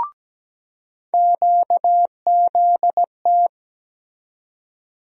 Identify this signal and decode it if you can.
Morse code, heard as a keyed tone, about 17 wpm, 700 Hz. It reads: QZT
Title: USA Call Sign Postfixes - 3 Letters 17wpm